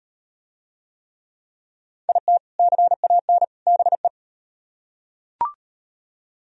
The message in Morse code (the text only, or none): IT CAN BE